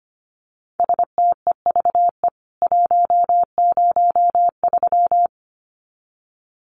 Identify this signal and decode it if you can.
Morse code, heard as a keyed tone, about 25 wpm, 700 Hz. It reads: STEVE 103
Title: CWT Exchange 25wpm